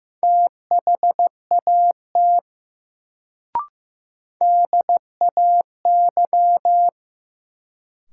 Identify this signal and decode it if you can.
Morse code, heard as a keyed tone, about 15 wpm, 700 Hz. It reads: THAT DAY
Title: Sets of 2 Words - New Word Drill - 15wpm